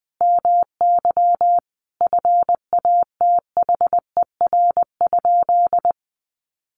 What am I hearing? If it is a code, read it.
Morse code, heard as a keyed tone, about 20 wpm, 700 Hz. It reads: MY FATHER?
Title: Sentences from Top 400 Words 20wpm